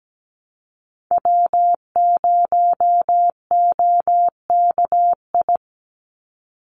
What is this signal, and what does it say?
Morse code, heard as a keyed tone, about 17 wpm, 700 Hz. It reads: W0OKI
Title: USA Call Signs 17wpm